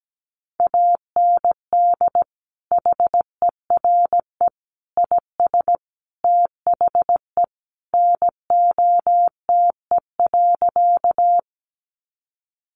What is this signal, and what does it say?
Morse code, heard as a keyed tone, about 17 wpm, 700 Hz. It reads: AND HERE IS THE NOTE.
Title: Sentences from Top 500 Words 17wpm